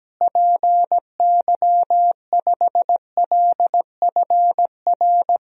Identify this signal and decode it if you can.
Morse code, heard as a keyed tone, about 17 wpm, 700 Hz. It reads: PY5LFR
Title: DXCC-100 Call Signs - 6 Characters - Repeated 2x - 17wpm